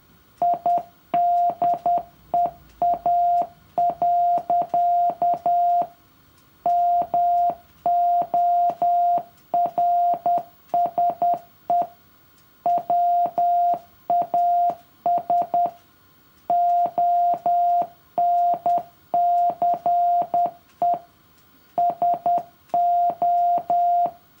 Morse code, heard as a keyed tone, about 10 wpm, 700 Hz. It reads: IDEA. MORSE WAS ONCE SO
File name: morse-code-for-year-3